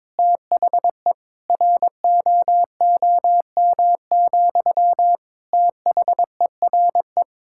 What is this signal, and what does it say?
Morse code, heard as a keyed tone, about 22 wpm, 700 Hz. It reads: THE ROOM, THERE
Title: Sentences from Top 1000 Words - Review Part 6 - 22wpm